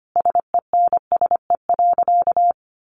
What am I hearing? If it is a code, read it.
Morse code, heard as a keyed tone, about 25 wpm, 700 Hz. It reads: SENSE.
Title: Sentences from Top 1500 Words 25wpm